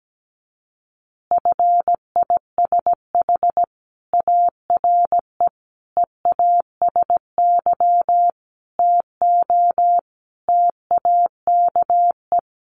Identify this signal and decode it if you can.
Morse code, heard as a keyed tone, about 17 wpm, 700 Hz. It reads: FISH ARE EASY TO TAKE
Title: Easy Phrases 17wpm